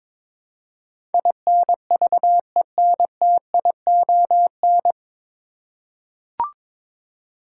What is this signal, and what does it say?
Morse code, heard as a keyed tone, about 22 wpm, 700 Hz. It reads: INVENTION
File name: 9 Letter Words - First 3 Letters Given - 22wpm